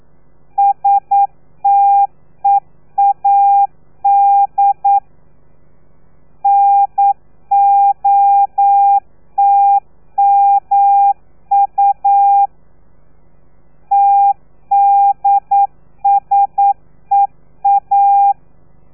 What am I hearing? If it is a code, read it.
Morse code, heard as a keyed tone, about 9 wpm, 800 Hz. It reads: STEAD NOTMU TDSEA
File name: telegrafia / i2tfj / Lez3 40